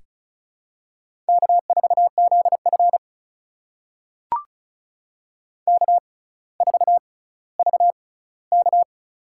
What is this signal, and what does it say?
Morse code, heard as a keyed tone, about 36 wpm, spaced as for 18 wpm, 700 Hz. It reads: K4ZF K4VK
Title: USA Call Signs - 4 Characters - 6x Character Spacing - 35wpm